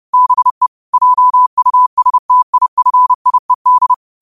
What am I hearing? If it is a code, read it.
Morse code, heard as a keyed tone, about 30 wpm, 1000 Hz. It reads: BE JUSTIFIED
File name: qotdfiles / 30wpm